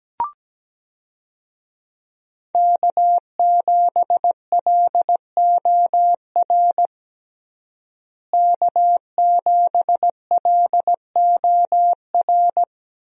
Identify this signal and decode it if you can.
Morse code, heard as a keyed tone, about 17 wpm, 700 Hz. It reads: K7LOR K7LOR
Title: USA Call Signs - 5 Characters - Repeated 2x - 17wpm